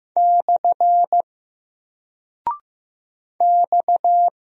Morse code, heard as a keyed tone, about 15 wpm, 700 Hz. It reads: / X
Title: Single Character (Letter:Number:Punctuation:Prosign) 15wpm